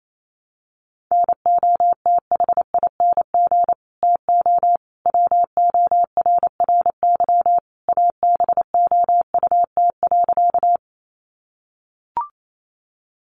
Morse code, heard as a keyed tone, about 28 wpm, 700 Hz. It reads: NOTHING TO WORRY ABOUT.
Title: Sentences from Top 1100 Words 28wpm